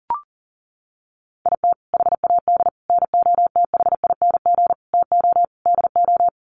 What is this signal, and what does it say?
Morse code, heard as a keyed tone, about 40 wpm, 700 Hz. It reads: IT HAD NOTHING TO DO
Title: Sets of 5 Words Encore2 40wpm